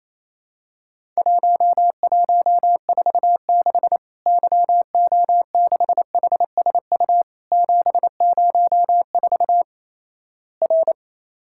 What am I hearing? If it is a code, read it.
Morse code, heard as a keyed tone, about 28 wpm, 700 Hz. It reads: 1146 YO6HSU 704 R
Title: Worked All Europe DX Contest - QTC Exchange - 28wpm